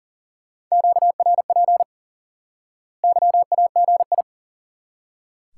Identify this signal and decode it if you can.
Morse code, heard as a keyed tone, about 40 wpm, 700 Hz. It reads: QRP YAGI
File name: IQE Course 3 - Lesson 89 - YAGI - Two Elements - 40wpm